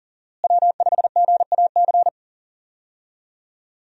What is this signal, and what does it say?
Morse code, heard as a keyed tone, about 38 wpm, 700 Hz. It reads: W5GAC
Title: USA Call Signs - 5 Characters - Repeated 2x - 40wpm